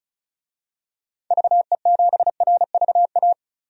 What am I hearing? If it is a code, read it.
Morse code, heard as a keyed tone, about 34 wpm, 700 Hz. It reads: VE7RVA
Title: DXCC-100 Call Signs - 6 Characters - 35wpm